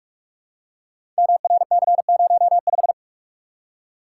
Morse code, heard as a keyed tone, about 44 wpm, 700 Hz. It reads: MRC05
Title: Code Groups - 5 Letter:Numbers - 45wpm